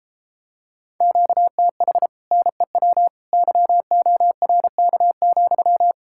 Morse code, heard as a keyed tone, about 33 wpm, 700 Hz. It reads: QTH NEW YORK,